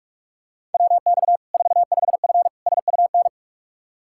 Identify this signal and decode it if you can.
Morse code, heard as a keyed tone, about 43 wpm, 700 Hz. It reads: WX 45F SUN
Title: Practice QSO - Exchange 2 - Info - 45wpm